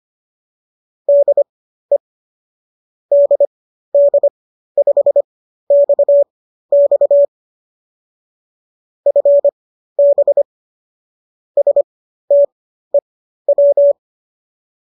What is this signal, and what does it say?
Morse code, heard as a keyed tone, about 25 wpm, spaced as for 13 wpm, 570 Hz. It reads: DE DD5XX FB STEW